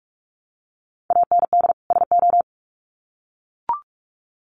Morse code, heard as a keyed tone, about 44 wpm, 700 Hz. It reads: AND SO
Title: Sets of 2 Words 45wpm